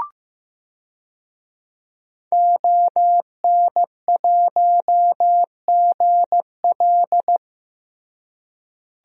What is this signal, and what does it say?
Morse code, heard as a keyed tone, about 15 wpm, 700 Hz. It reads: ON1GL
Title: DXCC-100 Call Signs - 5 Characters - Repeated 2x - 15wpm